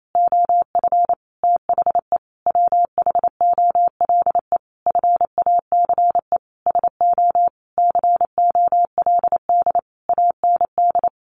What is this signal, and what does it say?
Morse code, heard as a keyed tone, about 28 wpm, 700 Hz. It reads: OF THE WHOLE FACE SO COLD AND
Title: Sentences from Top 1400 Words 28wpm